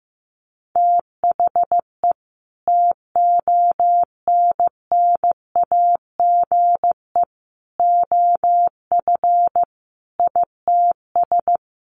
Morse code, heard as a keyed tone, about 15 wpm, 700 Hz. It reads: THE TONNAGE OF ITS
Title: A Sound Of Thunder 15wpm